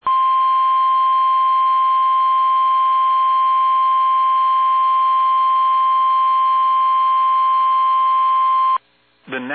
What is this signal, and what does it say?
Emergency Alert System (SAME) tones (the 1050 Hz attention tone), with speech. It is heard from 0.0 to 8.8 s.
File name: eas / KUSO / KUSO